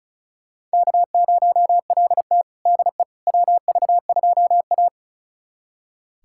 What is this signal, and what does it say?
Morse code, heard as a keyed tone, about 36 wpm, 700 Hz. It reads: K0LT DE WV2A
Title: USA Callsign DE Callsign - 4 Characters - 35wpm